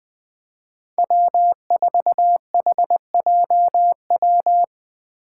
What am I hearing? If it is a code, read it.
Morse code, heard as a keyed tone, about 20 wpm, 700 Hz. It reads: W4HJW
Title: Five Challenging Letter-Number Combinations 20wpm